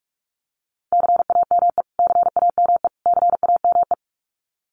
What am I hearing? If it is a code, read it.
Morse code, heard as a keyed tone, about 44 wpm, 700 Hz. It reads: CAME CAME CAME
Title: Top 500 Words Repeated 45wpm